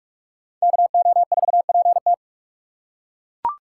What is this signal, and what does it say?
Morse code, heard as a keyed tone, about 41 wpm, 700 Hz. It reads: KO4PT